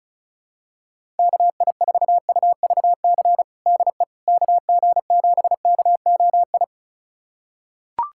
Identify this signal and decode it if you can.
Morse code, heard as a keyed tone, about 34 wpm, 700 Hz. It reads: KI4UVC DE KG7KOI
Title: USA Callsign DE Callsign - 6 Characters - 35wpm